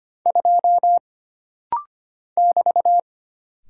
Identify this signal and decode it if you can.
Morse code, heard as a keyed tone, about 25 wpm, 700 Hz. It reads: 2 <BT>
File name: Single Character (Letter:Number:Punctuation:Prosign) - Rapid-Fire - 25wpm